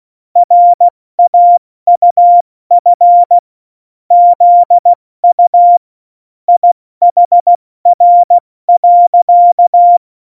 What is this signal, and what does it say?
Morse code, heard as a keyed tone, about 16 wpm, 700 Hz. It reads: RAUF ZU IHR.
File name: Morsekurs / Witze / Witz 29. Dezember 2020, 70 F80